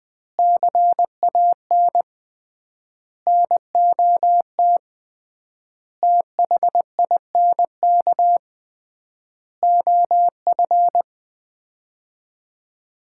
Morse code, heard as a keyed tone, about 20 wpm, 700 Hz. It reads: CAN NOT THINK OF
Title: 3x Word-Spacing - Sets of 5 Words - 20wpm